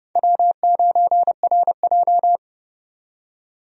Morse code, heard as a keyed tone, about 30 wpm, 700 Hz. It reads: W9RJ
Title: USA Call Signs - 4 Characters - 30wpm